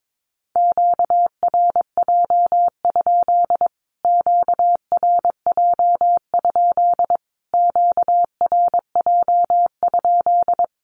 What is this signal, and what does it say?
Morse code, heard as a keyed tone, about 22 wpm, 700 Hz. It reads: QRJ? QRJ? QRJ?
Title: Q-Codes Repeated 3x 22wpm